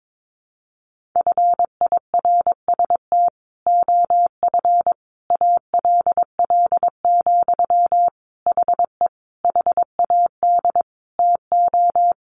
Morse code, heard as a keyed tone, about 22 wpm, 700 Hz. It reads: FIRST OF ALL, HE HAD TO
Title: Sentences from Top 1200 Words 22wpm